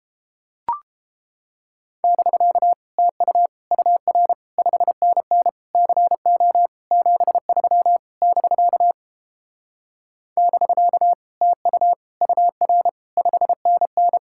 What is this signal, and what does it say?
Morse code, heard as a keyed tone, about 29 wpm, 700 Hz. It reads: <BK> TU UR 5NN CO 73 <BK> <BK> TU UR 5NN